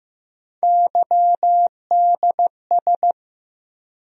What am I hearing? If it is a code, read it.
Morse code, heard as a keyed tone, about 15 wpm, 700 Hz. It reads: YDS
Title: USA Call Sign Postfixes - 3 Letters 15wpm